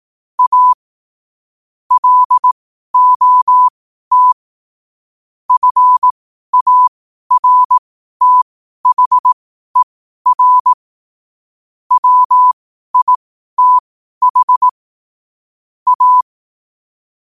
Morse code, heard as a keyed tone, about 18 wpm, spaced as for 12 wpm, 1000 Hz. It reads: A LOT FARTHER WITH A
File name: qotdfiles / 12wpm